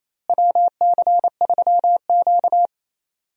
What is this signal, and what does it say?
Morse code, heard as a keyed tone, about 28 wpm, 700 Hz. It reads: WC3Q
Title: USA Call Signs - 4 Characters - Repeated 2x - 28wpm